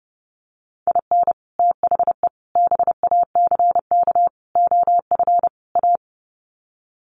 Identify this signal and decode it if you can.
Morse code, heard as a keyed tone, about 30 wpm, 700 Hz. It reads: IN THE BACK OF A
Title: Sets of 5 Words Encore2 30wpm